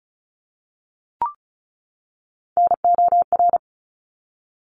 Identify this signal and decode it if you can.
Morse code, heard as a keyed tone, about 34 wpm, 700 Hz. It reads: NOR
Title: Top 1251-1500 Words 35wpm